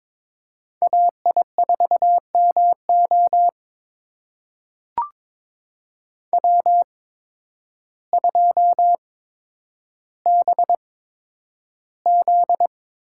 Morse code, heard as a keyed tone, about 22 wpm, spaced as for 10 wpm, 700 Hz. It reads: AI4MO W2BZ
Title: USA Call Signs - 8x Character Spacing - 22wpm